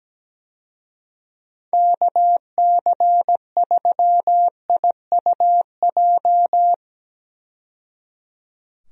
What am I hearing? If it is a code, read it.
Morse code, heard as a keyed tone, about 17 wpm, 700 Hz. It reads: KC3IUJ
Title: USA Call Signs - 6 Characters - 17wpm